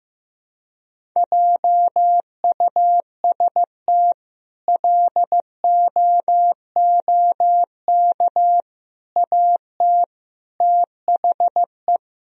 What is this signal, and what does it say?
Morse code, heard as a keyed tone, about 15 wpm, 700 Hz. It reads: JUST LOOK AT THE
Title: Sentences from Top 1000 Words - Review Part 1 - 15wpm